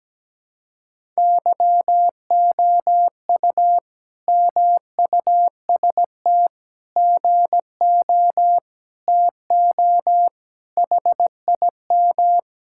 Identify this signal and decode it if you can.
Morse code, heard as a keyed tone, about 17 wpm, 700 Hz. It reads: YOU MUST GO TO HIM
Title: Sentences from Top 1000 Words - Review Part 5 - 17wpm